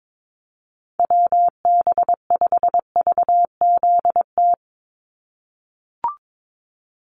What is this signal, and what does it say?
Morse code, heard as a keyed tone, about 22 wpm, 700 Hz. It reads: WB5VZT
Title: USA Call Signs 22wpm